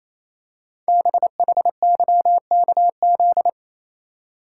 Morse code, heard as a keyed tone, about 28 wpm, 700 Hz. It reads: BHYKZ
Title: Code Groups - 5 Letters - 28wpm